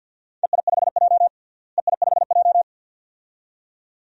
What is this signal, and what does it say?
Morse code, heard as a keyed tone, about 51 wpm, 700 Hz. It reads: EI5J EI5J
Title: DXCC-100 Call Signs - 4 Characters - Repeated 2x - 50wpm